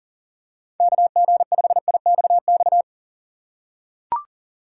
Morse code, heard as a keyed tone, about 40 wpm, 700 Hz. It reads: KG5IXX
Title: USA Call Signs - 6 Characters - 2x Character Spacing - 40wpm